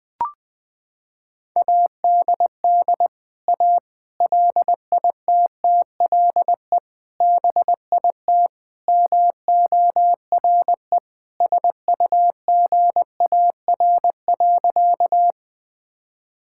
Morse code, heard as a keyed tone, about 20 wpm, 700 Hz. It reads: ADD A LITTLE BIT MORE SUGAR.
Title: Sentences with 4-Word Phrases - 20wpm